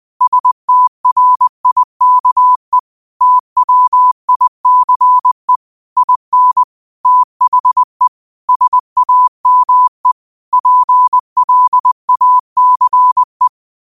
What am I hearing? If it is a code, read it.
Morse code, heard as a keyed tone, about 20 wpm, 1000 Hz. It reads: STRIKE TWICE IN THE SAME PLACE
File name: qotdfiles / 20wpm